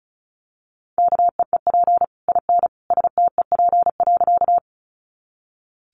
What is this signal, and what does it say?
Morse code, heard as a keyed tone, about 34 wpm, 700 Hz. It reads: KEEP IN STEP.
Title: Sentences from Top 400 Words 35wpm